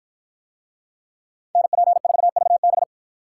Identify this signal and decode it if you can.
Morse code, heard as a keyed tone, about 53 wpm, 700 Hz. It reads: NP4VB